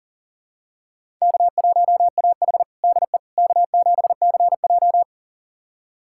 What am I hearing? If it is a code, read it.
Morse code, heard as a keyed tone, about 40 wpm, 700 Hz. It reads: K1AH DE K7CJ
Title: USA Callsign DE Callsign - 4 Characters - 40wpm